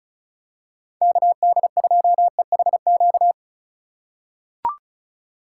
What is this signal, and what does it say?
Morse code, heard as a keyed tone, about 34 wpm, 700 Hz. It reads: KD2EHQ
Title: USA Call Signs - 2x Character Spacing - 35wpm